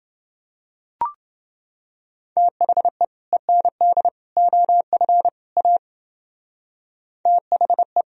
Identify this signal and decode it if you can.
Morse code, heard as a keyed tone, about 29 wpm, 700 Hz. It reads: THE END OF A THE